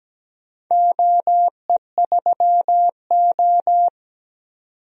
Morse code, heard as a keyed tone, about 17 wpm, 700 Hz. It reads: OE3O
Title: DXCC-100 Call Signs - 4 Characters - 17wpm